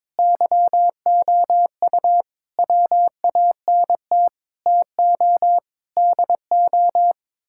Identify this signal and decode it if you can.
Morse code, heard as a keyed tone, about 21 wpm, 700 Hz. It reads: YOU WANT TO DO
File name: practice / 22wpm / Sets of 7 Words - 22wpm